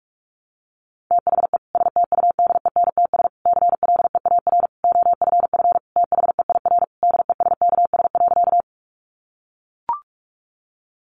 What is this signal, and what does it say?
Morse code, heard as a keyed tone, about 44 wpm, 700 Hz. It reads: THE STUDENTS CLEAR OFF THEIR DESKS.
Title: Additional Easy Phrases 45wpm